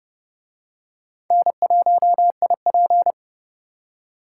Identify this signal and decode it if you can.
Morse code, heard as a keyed tone, about 28 wpm, 700 Hz. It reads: N1IP